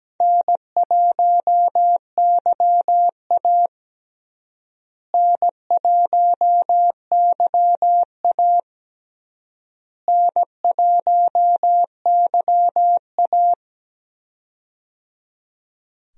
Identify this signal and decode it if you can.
Morse code, heard as a keyed tone, about 17 wpm, 700 Hz. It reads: N1YA N1YA N1YA
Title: USA Call Signs - 4 Characters - Repeated 3x - 17wpm